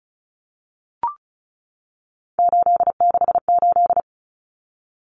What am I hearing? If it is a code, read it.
Morse code, heard as a keyed tone, about 34 wpm, 700 Hz. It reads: 868
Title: Numbers - 3-Digit 35wpm